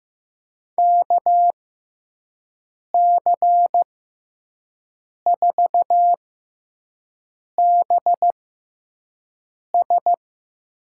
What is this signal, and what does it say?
Morse code, heard as a keyed tone, about 15 wpm, spaced as for 5 wpm, 700 Hz. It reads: KC4BS
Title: USA Call Signs - 6 Characters - 6x Character Spacing - 15wpm